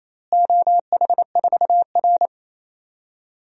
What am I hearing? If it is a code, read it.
Morse code, heard as a keyed tone, about 28 wpm, 700 Hz. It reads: OH4R
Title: DXCC-100 Call Signs - 4 Characters - Repeated 3x - 28wpm